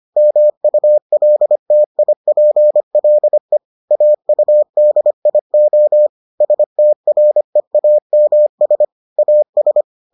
Morse code, heard as a keyed tone, about 25 wpm, 600 Hz. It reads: MULTIPLE AUDIO STREAMS AS